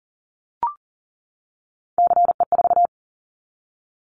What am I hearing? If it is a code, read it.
Morse code, heard as a keyed tone, about 40 wpm, 700 Hz. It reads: CE4
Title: Callsign Prefixes for Top 100 DXCC - 40wpm